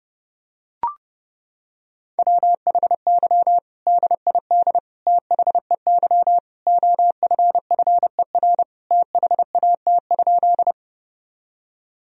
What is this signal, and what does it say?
Morse code, heard as a keyed tone, about 32 wpm, 700 Hz. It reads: WHY DID THEY OFFER THAT?
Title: Sentences from Top 1000 Words - Review Part 4 - 30wpm